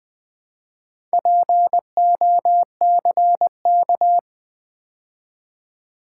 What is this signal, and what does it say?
Morse code, heard as a keyed tone, about 20 wpm, 700 Hz. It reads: POCK
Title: Build A Word - 2 Letters At A Time - 6-Letter Words - 20wpm